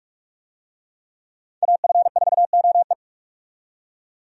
Morse code, heard as a keyed tone, about 43 wpm, 700 Hz. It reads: AF4OE
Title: USA Call Signs - 5 Characters - 45wpm